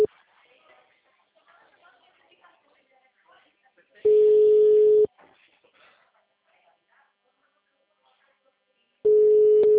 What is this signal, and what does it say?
Ringback tone (Europe, 425 Hz), with speech.